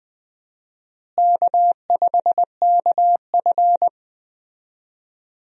Morse code, heard as a keyed tone, about 20 wpm, 700 Hz. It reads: K5KF
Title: USA Call Signs - 4 Characters - Repeated 3x - 20wpm